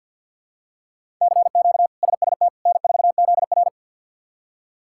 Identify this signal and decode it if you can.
Morse code, heard as a keyed tone, about 52 wpm, 700 Hz. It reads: CQ SST N4ZR
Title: K1USN SST QSO Practice - 50wpm